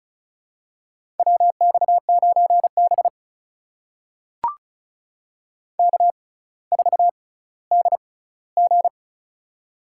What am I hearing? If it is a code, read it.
Morse code, heard as a keyed tone, about 36 wpm, spaced as for 18 wpm, 700 Hz. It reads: WX9B K4DG
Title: USA Call Signs - 4 Characters - 6x Character Spacing - 35wpm